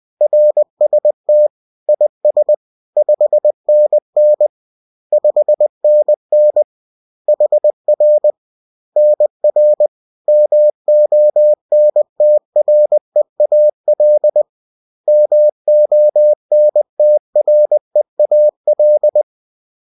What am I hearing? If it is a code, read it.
Morse code, heard as a keyed tone, about 20 wpm, 600 Hz. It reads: RST IS 5NN 5NN HR NR MONTREAL MONTREAL